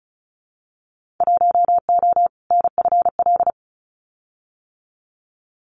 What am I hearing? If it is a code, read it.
Morse code, heard as a keyed tone, about 36 wpm, 700 Hz. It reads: 1O NFL
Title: Winter Field Day Exchanges 35wpm